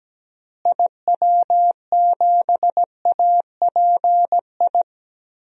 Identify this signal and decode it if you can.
Morse code, heard as a keyed tone, about 17 wpm, 700 Hz. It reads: IW7API
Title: DXCC-100 Call Signs - 6 Characters - Repeated 3x - 17wpm